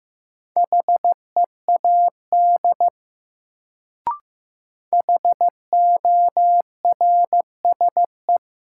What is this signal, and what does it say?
Morse code, heard as a keyed tone, about 15 wpm, 700 Hz. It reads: HEAD HORSE
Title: Sentences from Top 300 Words - New Word Drill - 15wpm